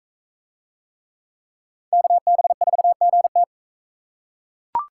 Morse code, heard as a keyed tone, about 41 wpm, 700 Hz. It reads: KB4GT